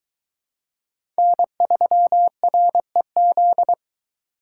Morse code, heard as a keyed tone, about 22 wpm, 700 Hz. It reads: N3REZ